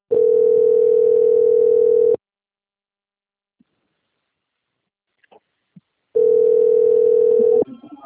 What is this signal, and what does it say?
Ringback tone (North America), 440 + 480 Hz.
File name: out- - - - - .77785